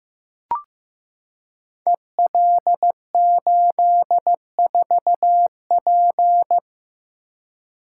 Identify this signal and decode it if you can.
Morse code, heard as a keyed tone, about 15 wpm, 700 Hz. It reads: EL84P
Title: Code Groups - 5 Letter:Numbers - 15wpm